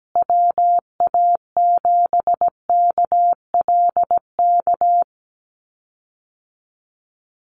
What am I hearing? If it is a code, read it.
Morse code, heard as a keyed tone, about 17 wpm, 700 Hz. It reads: WA7KLK
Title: USA Call Signs Repeated 3x 17wpm